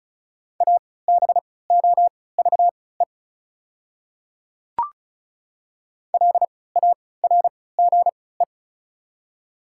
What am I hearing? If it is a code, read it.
Morse code, heard as a keyed tone, about 36 wpm, spaced as for 20 wpm, 700 Hz. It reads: ABOVE LARGE
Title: 3x Character Spacing - Top 500 Words - 35wpm